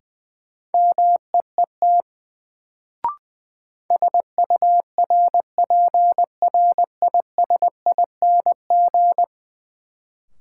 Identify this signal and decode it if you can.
Morse code, heard as a keyed tone, about 21 wpm, 700 Hz. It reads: MEET SURPRISING
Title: Sets of 5 Words - New Word Drill - 20wpm